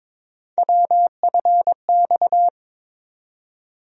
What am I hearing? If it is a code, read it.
Morse code, heard as a keyed tone, about 22 wpm, 700 Hz. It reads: WFX
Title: USA Call Sign Postfixes - 3 Letters 22wpm